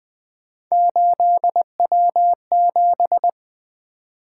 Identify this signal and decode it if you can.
Morse code, heard as a keyed tone, about 24 wpm, 700 Hz. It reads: 8W7
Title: Speed-Racing - Three Letter-Number Combinations - 20wpm